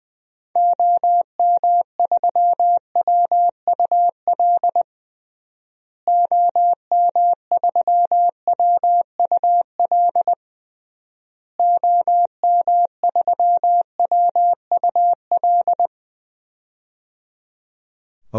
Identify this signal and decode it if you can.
Morse code, heard as a keyed tone, about 21 wpm, 700 Hz. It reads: OM3WUL OM3WUL OM3WUL
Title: DXCC-100 Call Signs - 6 Characters - Repeated 3x - 20wpm